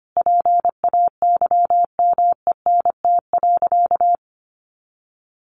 Morse code, heard as a keyed tone, about 25 wpm, 700 Hz. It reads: PAYMENT.
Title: Sentences from Top 1200 Words 25wpm